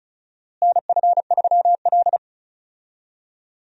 Morse code, heard as a keyed tone, about 34 wpm, 700 Hz. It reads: NF3L
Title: USA Call Signs - 4 Characters - Repeated 2x - 35wpm